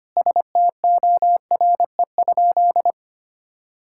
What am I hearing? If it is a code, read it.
Morse code, heard as a keyed tone, about 25 wpm, 700 Hz. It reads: STORE?
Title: Sentences from Top 1000 Words - Review Part 6 - 25wpm